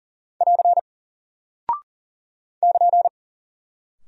Morse code, heard as a keyed tone, about 39 wpm, 700 Hz. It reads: <AR> <KN>
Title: Single Character (Letter:Number:Punctuation:Prosign) Alt 40wpm